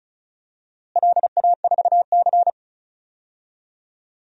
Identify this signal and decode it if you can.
Morse code, heard as a keyed tone, about 36 wpm, 700 Hz. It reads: LA4C
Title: DXCC-100 Call Signs - 4 Characters - Repeated 2x - 35wpm